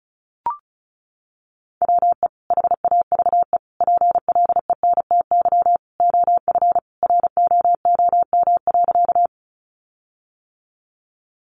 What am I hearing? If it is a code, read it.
Morse code, heard as a keyed tone, about 34 wpm, 700 Hz. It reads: WE HAVE PLENTY OF ROOM.
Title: Sentences from Top 1300 Words 35wpm